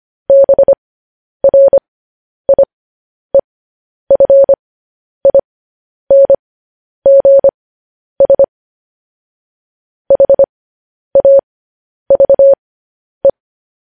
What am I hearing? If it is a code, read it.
Morse code, heard as a keyed tone, about 25 wpm, spaced as for 10 wpm, 550 Hz. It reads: BRIEFINGS HAVE